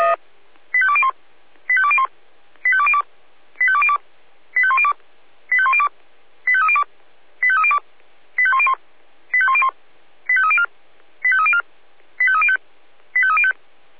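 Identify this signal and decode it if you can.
Five-tone selective call (ZVEI), heard as quick runs of five tones, tones of about 70 ms. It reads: 85201 85201 85202 85202 85102 85102 85302 85302 85101 85101 85304 85304 85305 85305
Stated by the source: ZVEI five-tone selective call received on SDR.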